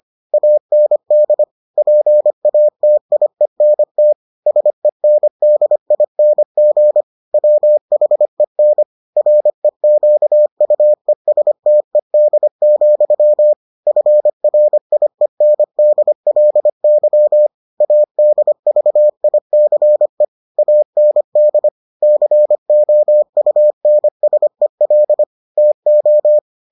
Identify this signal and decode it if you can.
Morse code, heard as a keyed tone, about 25 wpm, 600 Hz. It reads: AND PATIENT SENDING WHEN REQUESTED, FRIENDLY ADVICE AND COUNSEL TO